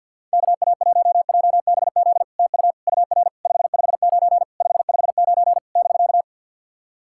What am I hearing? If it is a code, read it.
Morse code, heard as a keyed tone, about 50 wpm, 700 Hz. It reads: KA1JBZ TU UR 559 559 <BK>
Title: SOTA QSO Practice 50wpm